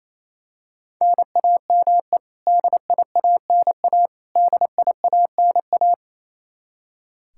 Morse code, heard as a keyed tone, about 28 wpm, 700 Hz. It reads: NAME DIANA DIANA
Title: Practice QSO - Exchange 1 - Beginning - 28wpm